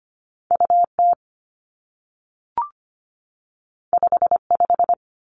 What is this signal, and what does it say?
Morse code, heard as a keyed tone, about 25 wpm, 700 Hz. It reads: U T 5 5
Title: Two Challenging Letter-Number Combinations 25wpm